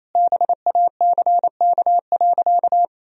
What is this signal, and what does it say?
Morse code, heard as a keyed tone, about 28 wpm, 700 Hz. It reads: BACK.
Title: Sentences from Top 1000 Words - Review Part 2 - 28wpm